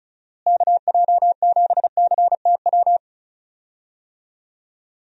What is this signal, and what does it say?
Morse code, heard as a keyed tone, about 34 wpm, 700 Hz. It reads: KJ7CTW
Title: USA Call Signs - 6 Characters - Repeated 3x - 35wpm